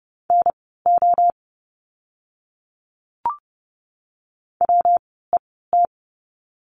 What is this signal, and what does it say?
Morse code, heard as a keyed tone, about 30 wpm, spaced as for 17 wpm, 700 Hz. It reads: NO WET
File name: course / 30:10wpm / 115 Lesson 19f - Letter w - Words 30wpm